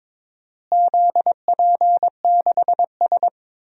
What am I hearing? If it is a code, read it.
Morse code, heard as a keyed tone, about 21 wpm, 700 Hz. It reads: ZP6S
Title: DXCC-100 Call Signs - 4 Characters - Repeated 3x - 22wpm